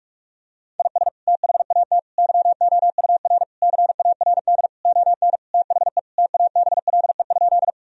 Morse code, heard as a keyed tone, about 41 wpm, 700 Hz. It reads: IS THAT YOUR CARD ON THE TABLE?
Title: Sentences from Top 1000 Words - Review Part 4 - 45wpm